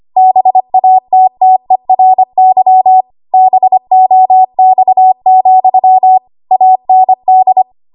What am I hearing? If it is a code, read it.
Morse code, heard as a keyed tone, about 25 wpm, 750 Hz. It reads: BATTERY BOX, AND
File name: Morse / 25WPM